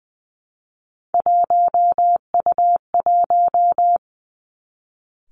Morse code, heard as a keyed tone, about 20 wpm, 700 Hz. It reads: 1U1
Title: Three Challenging Letter-Number Combinations 20wpm